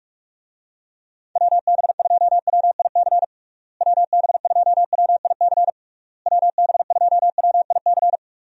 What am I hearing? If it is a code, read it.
Morse code, heard as a keyed tone, about 46 wpm, 700 Hz. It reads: WB2WIC WB2WIC WB2WIC
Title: USA Call Signs - 6 Characters - Repeated 3x - 45wpm